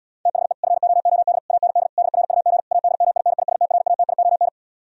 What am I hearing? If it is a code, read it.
Morse code, heard as a keyed tone, about 77 wpm, 700 Hz. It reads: THE FOOD AND DRUG ADMINISTRATION